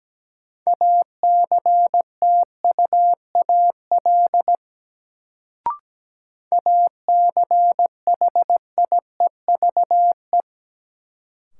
Morse code, heard as a keyed tone, about 17 wpm, 700 Hz. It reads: ACTUAL ACHIEVE
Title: Sentences from Top 1000 Words - Review Part 5 - New Word Drill - 17wpm